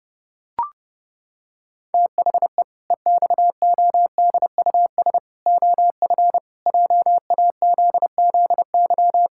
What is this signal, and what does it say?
Morse code, heard as a keyed tone, about 30 wpm, 700 Hz. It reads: THE EXODUS OF JAZZY
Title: Pangram Challenge v2 - 30wpm